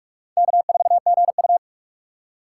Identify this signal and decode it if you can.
Morse code, heard as a keyed tone, about 45 wpm, 700 Hz. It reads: K4GU
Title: USA Call Signs - 4 Characters - 6x Character Spacing - 45wpm